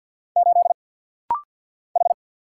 Morse code, heard as a keyed tone, about 50 wpm, 700 Hz. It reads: 8 H
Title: Single Letter-Number - Rapid-Fire 50wpm